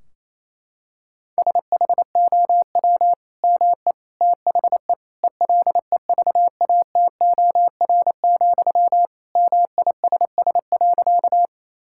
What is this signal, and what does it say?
Morse code, heard as a keyed tone, about 28 wpm, 700 Hz. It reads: SHOW ME THE ELEVATOR, MISS.
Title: Sentences from Top 1600 Words 28wpm